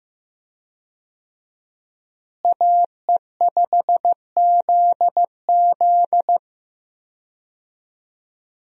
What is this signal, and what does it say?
Morse code, heard as a keyed tone, about 15 wpm, 700 Hz. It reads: AE5ZZ
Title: USA Call Signs - 5 Characters - Repeated 2x - 15wpm